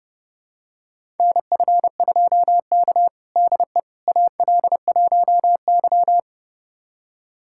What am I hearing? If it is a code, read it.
Morse code, heard as a keyed tone, about 30 wpm, 700 Hz. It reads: NF2K DE AL1Y
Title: USA Callsign DE Callsign - 4 Characters - 30wpm